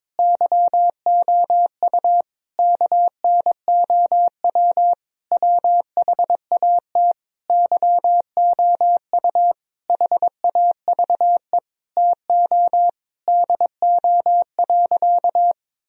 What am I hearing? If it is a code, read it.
Morse code, heard as a keyed tone, about 22 wpm, 700 Hz. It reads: YOU KNOW WHAT YOU HAVE TO DO.
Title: Sentences from Top 1000 Words - Review Part 3 - 22wpm